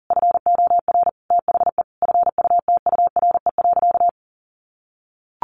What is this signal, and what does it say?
Morse code, heard as a keyed tone, about 40 wpm, 700 Hz. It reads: FOR THE FUTURE.
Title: Sentences from Top 1400 Words 40wpm